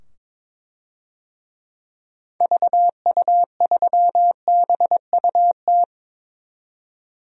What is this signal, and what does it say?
Morse code, heard as a keyed tone, about 22 wpm, 700 Hz. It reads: VU3BUT
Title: DXCC-100 Call Signs - 6 Characters - 22wpm